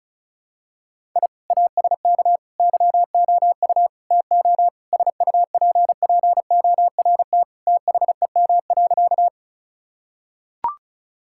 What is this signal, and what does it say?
Morse code, heard as a keyed tone, about 36 wpm, 700 Hz. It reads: I ASK YOU TO SUPPORT THEM.
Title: Sentences from Top 1000 Words - Review Part 5 - 35wpm